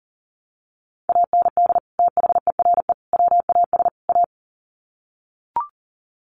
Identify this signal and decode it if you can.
Morse code, heard as a keyed tone, about 38 wpm, 700 Hz. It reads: AND THERE WAS A
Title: Sets of 4 Words 40wpm